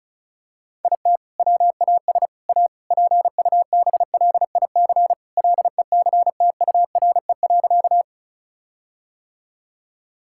Morse code, heard as a keyed tone, about 36 wpm, 700 Hz. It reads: IT WAS A PUBLIC LECTURE.
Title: Sentences from Top 1600 Words 35wpm